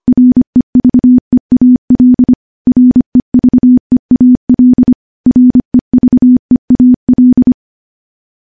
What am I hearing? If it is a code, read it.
Morse code, heard as a keyed tone, about 25 wpm, 250 Hz. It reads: REVEAL REVEAL REVEAL
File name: morse / morse-training-25WPM